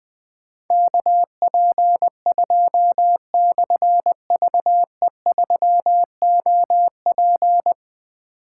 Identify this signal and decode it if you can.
Morse code, heard as a keyed tone, about 20 wpm, 700 Hz. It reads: KP2/VE3OP
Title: Difficult Call Signs 20wpm